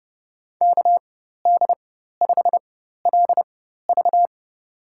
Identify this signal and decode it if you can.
Morse code, heard as a keyed tone, about 30 wpm, spaced as for 14 wpm, 700 Hz. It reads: KD5LV
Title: USA Call Signs - 5 Characters - 4x Character Spacing - 30wpm